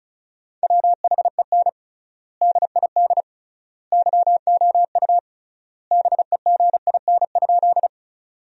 Morse code, heard as a keyed tone, about 36 wpm, 700 Hz. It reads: WHEN DID YOU BEGIN?
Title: 3x Word-Spacing - Sentences from Top 300 Words - 35wpm